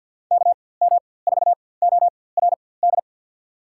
Morse code, heard as a keyed tone, about 48 wpm, spaced as for 23 wpm, 700 Hz. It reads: KM4ORD
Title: USA Call Signs - 4x Character Spacing - 50wpm